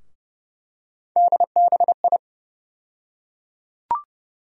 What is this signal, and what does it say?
Morse code, heard as a keyed tone, about 30 wpm, 700 Hz. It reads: DBI
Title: USA Call Sign Postfixes - 3 Letters 30wpm